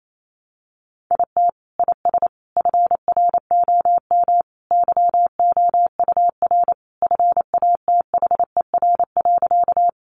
Morse code, heard as a keyed tone, about 28 wpm, 700 Hz. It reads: IT IS FROM YOUR FATHER.
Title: Sentences from Top 300 Words 28wpm